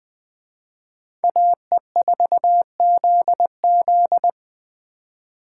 Morse code, heard as a keyed tone, about 20 wpm, 700 Hz. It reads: AE4ZZ